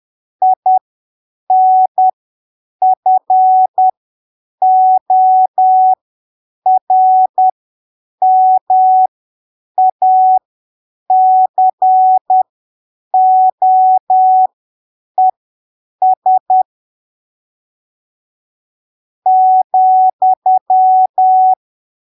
Morse code, heard as a keyed tone, about 10 wpm, spaced as for 7 wpm, 750 Hz. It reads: INFORMACOES ,
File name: arquivos / texto / texto 18 10ppm